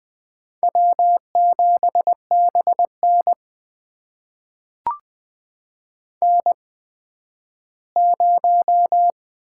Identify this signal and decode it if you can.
Morse code, heard as a keyed tone, about 20 wpm, spaced as for 12 wpm, 700 Hz. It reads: W7BN N0
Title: USA Call Signs - 4 Characters - 8x Character Spacing - 20wpm